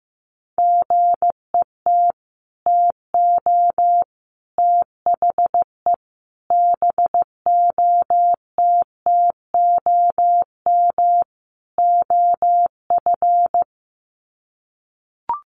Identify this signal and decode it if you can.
Morse code, heard as a keyed tone, about 15 wpm, 700 Hz. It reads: GET TO THE BOTTOM OF
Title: Sets of 5 Words Encore 15wpm